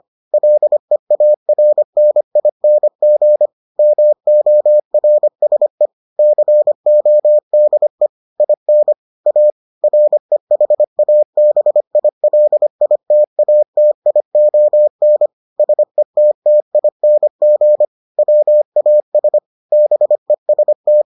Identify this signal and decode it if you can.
Morse code, heard as a keyed tone, about 25 wpm, 600 Hz. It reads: LEARNING MORSE CODE IN A REHABILITATION SETTING WAS BEST